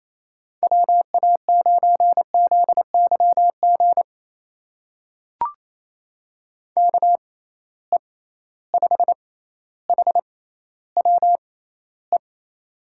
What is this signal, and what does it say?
Morse code, heard as a keyed tone, about 28 wpm, spaced as for 14 wpm, 700 Hz. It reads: WA9ZYG KE5HWE
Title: USA Call Signs - 6 Characters - 6x Character Spacing - 28wpm